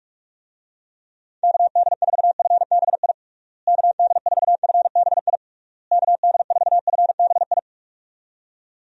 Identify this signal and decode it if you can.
Morse code, heard as a keyed tone, about 46 wpm, 700 Hz. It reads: KD4FBI KD4FBI KD4FBI
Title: USA Call Signs - 6 Characters - Repeated 3x - 45wpm